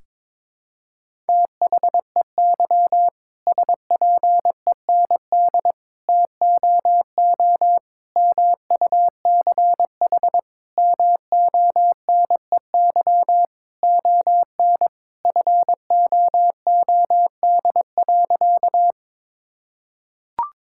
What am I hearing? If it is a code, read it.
Morse code, heard as a keyed tone, about 21 wpm, 700 Hz. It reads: THEY SPEND TOO MUCH MONEY ON FOOD.
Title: Sentences from Top 1000 Words - Review Part 5 - 22wpm